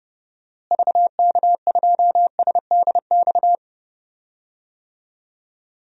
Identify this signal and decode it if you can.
Morse code, heard as a keyed tone, about 29 wpm, 700 Hz. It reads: VK2SDX
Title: DXCC-100 Call Signs - 6 Characters - Repeated 3x - 30wpm